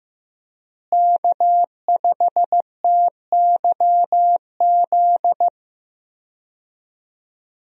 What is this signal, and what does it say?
Morse code, heard as a keyed tone, about 15 wpm, 700 Hz. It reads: K5TYZ
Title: USA Call Signs - 5 Characters - Repeated 3x - 15wpm